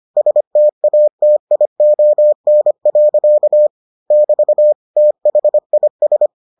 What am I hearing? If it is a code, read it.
Morse code, heard as a keyed tone, about 25 wpm, 600 Hz. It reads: STATION. <BT> THIS